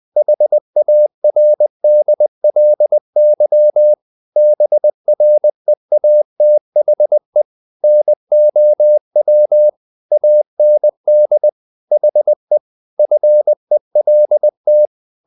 Morse code, heard as a keyed tone, about 20 wpm, 600 Hz. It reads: HARDLY BREATHE NOW AND HE FELT
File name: the old man and the sea part 2 20wpm